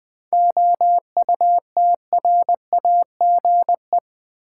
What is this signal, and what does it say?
Morse code, heard as a keyed tone, about 20 wpm, 700 Hz. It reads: OUTRAGE
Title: 7 Letter Words - First 3 Letters Given - 20wpm